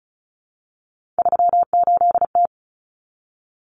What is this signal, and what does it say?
Morse code, heard as a keyed tone, about 34 wpm, 700 Hz. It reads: 38T
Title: Common Cut Numbers - 3-Digit 35wpm